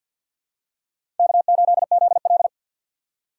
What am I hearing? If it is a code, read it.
Morse code, heard as a keyed tone, about 49 wpm, 700 Hz. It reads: K8ZL
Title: USA Call Signs - 4 Characters - 3x Character Spacing - 50wpm